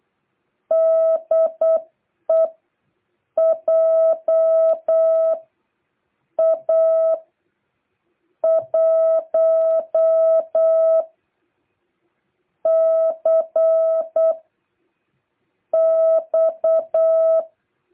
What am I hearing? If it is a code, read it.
Morse code, heard as a keyed tone, about 8 wpm, spaced as for 5 wpm, 660 Hz. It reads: DE JA1CX